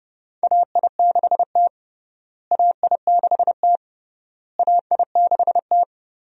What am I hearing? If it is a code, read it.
Morse code, heard as a keyed tone, about 32 wpm, 700 Hz. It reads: AI6T AI6T AI6T
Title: USA Call Signs - 4 Characters - Repeated 3x - 30wpm